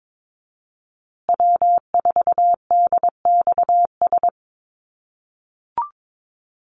Morse code, heard as a keyed tone, about 22 wpm, 700 Hz. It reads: W4DXS
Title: USA Call Signs Repeated 2x 22wpm